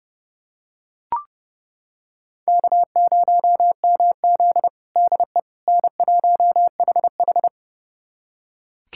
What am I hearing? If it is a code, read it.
Morse code, heard as a keyed tone, about 32 wpm, 700 Hz. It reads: K0MZ DE N1HH
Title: USA Callsign DE Callsign - 4 Characters - 30wpm